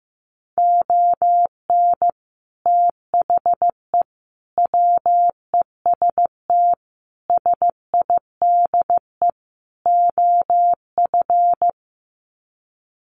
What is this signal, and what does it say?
Morse code, heard as a keyed tone, about 15 wpm, 700 Hz. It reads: ON THE WEST SIDE OF
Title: Sets of 5 Words Encore2 15wpm